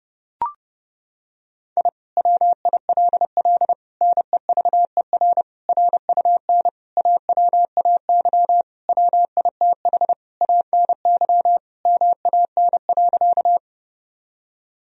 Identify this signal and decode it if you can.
Morse code, heard as a keyed tone, about 30 wpm, 700 Hz. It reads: I WILL NEVER RUN AWAY WITH ANY MAN.
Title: Sentences from Top 1000 Words - Review Part 1 - 30wpm